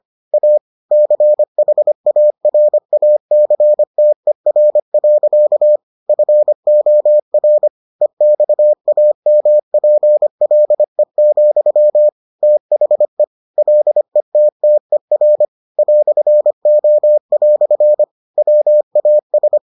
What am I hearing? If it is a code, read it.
Morse code, heard as a keyed tone, about 25 wpm, 600 Hz. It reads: A CHARACTER. FOR EXAMPLE, THE LETTER "O" WAS